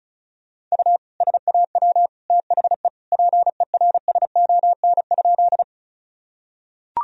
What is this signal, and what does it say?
Morse code, heard as a keyed tone, about 36 wpm, 700 Hz. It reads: U SAW THE PERSON?
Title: Sentences from Top 400 Words and Abbreviations 35wpm